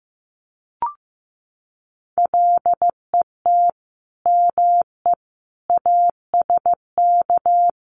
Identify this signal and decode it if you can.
Morse code, heard as a keyed tone, about 15 wpm, 700 Hz. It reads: LET ME ASK
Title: Sets of 5 Words 15wpm